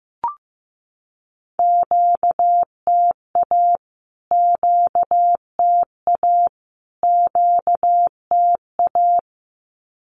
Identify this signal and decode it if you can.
Morse code, heard as a keyed tone, about 15 wpm, 700 Hz. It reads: QTA QTA QTA
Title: Q-Codes Repeated 3x 15wpm